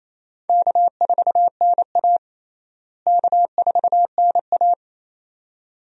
Morse code, heard as a keyed tone, about 28 wpm, 700 Hz. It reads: K4NA K4NA
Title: USA Call Signs - 4 Characters - Repeated 2x - 28wpm